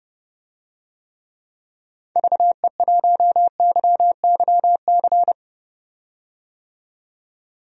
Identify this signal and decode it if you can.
Morse code, heard as a keyed tone, about 30 wpm, 700 Hz. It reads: VE1YYC
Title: DXCC-100 Call Signs - 6 Characters - 30wpm